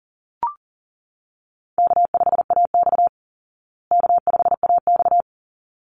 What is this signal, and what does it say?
Morse code, heard as a keyed tone, about 40 wpm, 700 Hz. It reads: K5AX K5AX
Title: Easy Call Signs Repeated 2x - 40wpm